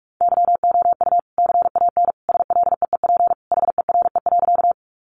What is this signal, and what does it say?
Morse code, heard as a keyed tone, about 44 wpm, 700 Hz. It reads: YOU CAN SLEEP HERE.
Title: Sentences from Top 500 Words 45wpm